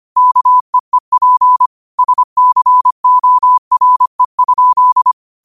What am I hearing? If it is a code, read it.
Morse code, heard as a keyed tone, about 25 wpm, 1000 Hz. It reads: KEEP SCORE?